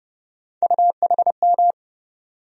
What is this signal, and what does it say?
Morse code, heard as a keyed tone, about 30 wpm, 700 Hz. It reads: UHM
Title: USA Call Sign Postfixes - 3 Letters 30wpm